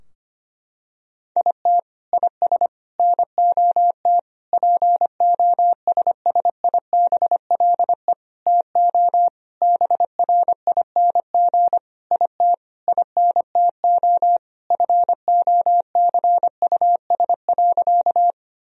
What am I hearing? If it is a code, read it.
Morse code, heard as a keyed tone, about 25 wpm, 700 Hz. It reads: IT IS NOT POSSIBLE TO BRING IT INTO FOCUS.
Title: Sentences from Top 1000 Words - Review Part 4 - 25wpm